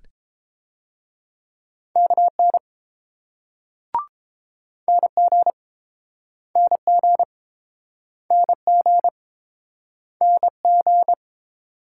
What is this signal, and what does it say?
Morse code, heard as a keyed tone, about 29 wpm, 700 Hz. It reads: KN NG NG NG NG